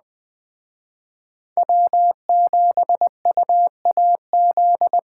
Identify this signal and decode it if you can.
Morse code, heard as a keyed tone, about 20 wpm, 700 Hz. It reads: W7UAZ